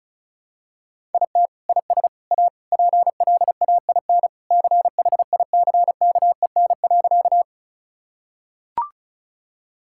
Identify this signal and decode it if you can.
Morse code, heard as a keyed tone, about 34 wpm, 700 Hz. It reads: IT IS A PLAIN CHICKEN.
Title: Sentences from Top 1000 Words - Review Part 5 - 35wpm